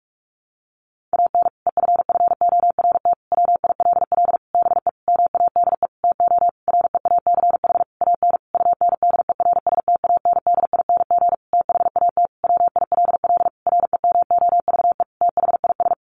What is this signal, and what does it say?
Morse code, heard as a keyed tone, about 44 wpm, 700 Hz. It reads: AN EFFORT WILL BE MADE TO REACH AN UNDERSTANDING THAT WILL REMOVE THIS
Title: Sentences from Top 1500 Words 45wpm